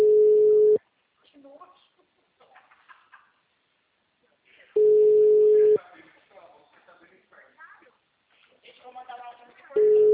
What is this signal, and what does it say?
Ringback tone (Europe, 425 Hz), with speech.
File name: out- -2019- - - .13768